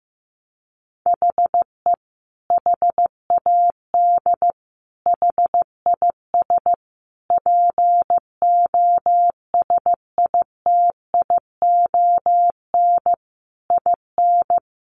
Morse code, heard as a keyed tone, about 15 wpm, 700 Hz. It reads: HE HAD HIS POSITION IN
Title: Sentences from Top 900 Words 15wpm